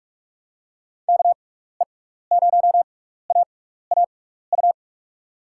Morse code, heard as a keyed tone, about 46 wpm, spaced as for 16 wpm, 700 Hz. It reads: KE0AAU
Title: USA Call Signs - 6 Characters - 6x Character Spacing - 45wpm